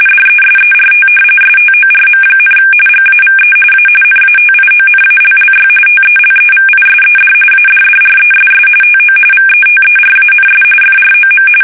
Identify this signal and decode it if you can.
RTTY, about 45.45 baud. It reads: DESERT STORM, GAVE A NATIONALLY TELEVISED ADDRESS TO VETERANS FROM
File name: fsk850 / battleshipnj